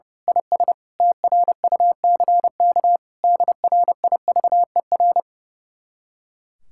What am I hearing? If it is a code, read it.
Morse code, heard as a keyed tone, about 28 wpm, 700 Hz. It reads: IS TRUCK DRIVER
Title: Speed-Racing Common QSO Phrases - 30wpm